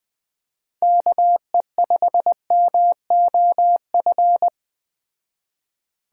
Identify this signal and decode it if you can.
Morse code, heard as a keyed tone, about 20 wpm, 700 Hz. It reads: KE5MOF